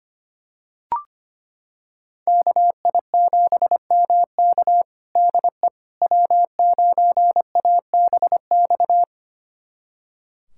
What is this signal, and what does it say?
Morse code, heard as a keyed tone, about 25 wpm, 700 Hz. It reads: KI7MK DE W9ABX
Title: USA Callsign DE Callsign - 5 Characters - 25wpm